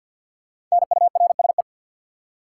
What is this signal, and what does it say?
Morse code, heard as a keyed tone, about 50 wpm, 700 Hz. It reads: NURSE
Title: Five Letter Words - 50wpm